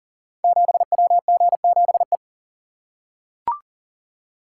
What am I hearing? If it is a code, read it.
Morse code, heard as a keyed tone, about 40 wpm, 700 Hz. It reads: 7WG7E
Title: Code Groups - 5 Letter:Numbers - 40wpm